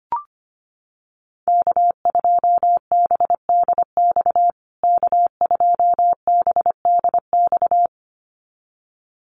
Morse code, heard as a keyed tone, about 25 wpm, 700 Hz. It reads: K2BDX K2BDX
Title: USA Call Signs Repeated 2x 25wpm